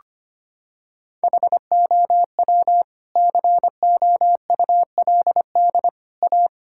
Morse code, heard as a keyed tone, about 25 wpm, 700 Hz. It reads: HOW COULD A
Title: Sentences from Top 1000 Words - Review Part 6 - 25wpm